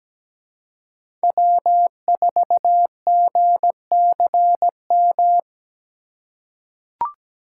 Morse code, heard as a keyed tone, about 17 wpm, 700 Hz. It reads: W4GCM
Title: USA Call Signs - 5 Characters - 4x Character Spacing - 17wpm